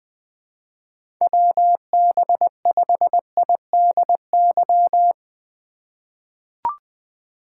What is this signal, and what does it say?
Morse code, heard as a keyed tone, about 21 wpm, 700 Hz. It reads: WB5IDY
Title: K1USN SST QSO Practice - 20wpm